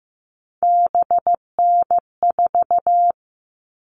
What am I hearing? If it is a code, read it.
Morse code, heard as a keyed tone, about 15 wpm, 700 Hz. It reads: BN4
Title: All Cut Numbers - 3-Digit 15wpm